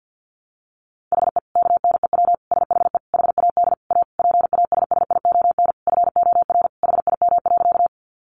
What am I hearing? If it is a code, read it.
Morse code, heard as a keyed tone, about 50 wpm, 700 Hz. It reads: HE KNEW SHE HAD A PASSION FOR HIM.
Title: Sentences from Top 1300 Words 50wpm